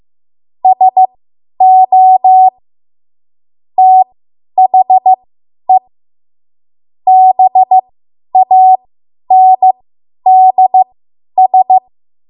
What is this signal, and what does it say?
Morse code, heard as a keyed tone, about 15 wpm, spaced as for 10 wpm, 750 Hz. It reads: SO THE BANDS
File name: Morse / 10WPM